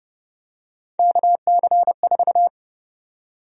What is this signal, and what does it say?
Morse code, heard as a keyed tone, about 30 wpm, 700 Hz. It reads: KC4
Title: USA Call Sign Prefixes - 30wpm